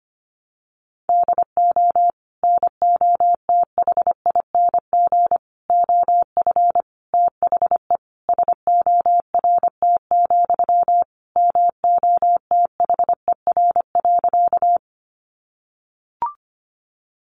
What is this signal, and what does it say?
Morse code, heard as a keyed tone, about 25 wpm, 700 Hz. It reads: DO NOTHING OF THE SORT, MOTHER.
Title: Sentences from Top 1300 Words 25wpm